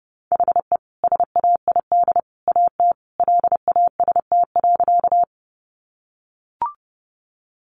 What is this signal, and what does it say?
Morse code, heard as a keyed tone, about 30 wpm, 700 Hz. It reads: HE SAID AT LAST.
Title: Sentences from Top 300 Words 30wpm